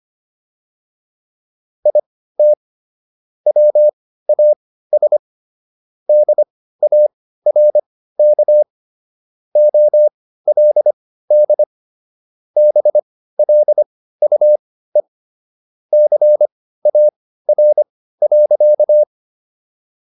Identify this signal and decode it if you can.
Morse code, heard as a keyed tone, about 25 wpm, spaced as for 15 wpm, 600 Hz. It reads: IT WAS DARK OLD BLUE CAR.